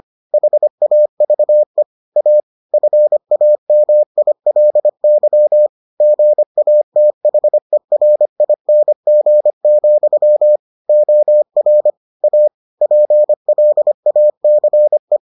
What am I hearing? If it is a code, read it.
Morse code, heard as a keyed tone, about 25 wpm, 600 Hz. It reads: HAVE A FAMILY GATHERING, OR A PLACE